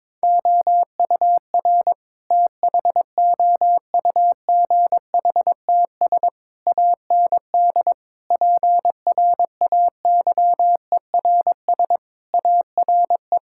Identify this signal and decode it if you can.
Morse code, heard as a keyed tone, about 22 wpm, 700 Hz. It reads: OUR THOUGHTS AND PRAYERS ARE
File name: practice / 22wpm / Sets of 7 Words - 22wpm